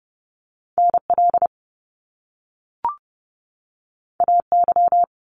Morse code, heard as a keyed tone, about 30 wpm, 700 Hz. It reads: NL AY
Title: Top Two Letter Combinations in Words 30wpm